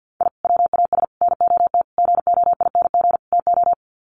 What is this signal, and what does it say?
Morse code, heard as a keyed tone, about 50 wpm, 700 Hz. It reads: I WAS NOT GOING TO
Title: Sets of 5 Words 50wpm